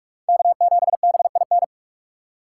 Morse code, heard as a keyed tone, about 43 wpm, 700 Hz. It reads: K7BIN